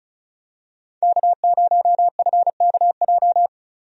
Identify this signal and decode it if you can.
Morse code, heard as a keyed tone, about 34 wpm, 700 Hz. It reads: K0FKJ
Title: USA Call Signs - 5 Characters - Repeated 2x - 35wpm